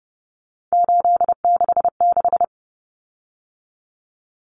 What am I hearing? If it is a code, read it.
Morse code, heard as a keyed tone, about 30 wpm, 700 Hz. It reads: 866
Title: Common Cut Numbers - 3-Digit 30wpm